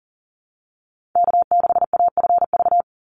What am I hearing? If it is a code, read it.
Morse code, heard as a keyed tone, about 40 wpm, 700 Hz. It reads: K6AFV
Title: USA Call Signs 40wpm